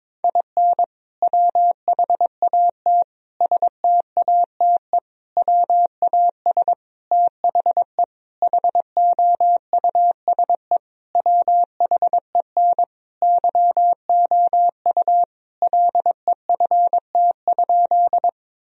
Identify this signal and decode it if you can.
Morse code, heard as a keyed tone, about 21 wpm, 700 Hz. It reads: IN WHAT STATE WAS THE HOUSE WHEN YOU LEFT?
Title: Sentences from Top 1000 Words - Review Part 2 - 22wpm